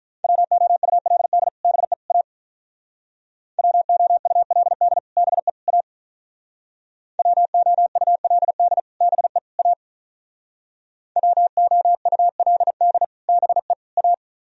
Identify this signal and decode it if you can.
Morse code, heard as a keyed tone, about 45 wpm, 700 Hz. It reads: WOULD BE A WOULD BE A WOULD BE A WOULD BE A